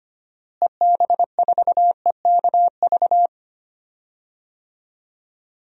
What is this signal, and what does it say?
Morse code, heard as a keyed tone, about 25 wpm, 700 Hz. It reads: EB4EKV
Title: DXCC-100 Call Signs - 6 Characters - Repeated 3x - 25wpm